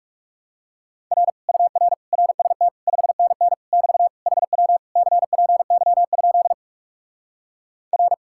Morse code, heard as a keyed tone, about 37 wpm, 700 Hz. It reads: R UR RST 5NN <BT> HW CPY? R
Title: Speed-Racing Common QSO Phrases - 30wpm